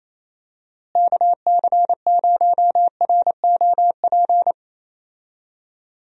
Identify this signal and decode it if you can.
Morse code, heard as a keyed tone, about 28 wpm, 700 Hz. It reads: KC0ROP
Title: USA Call Signs - 6 Characters - Repeated 2x - 28wpm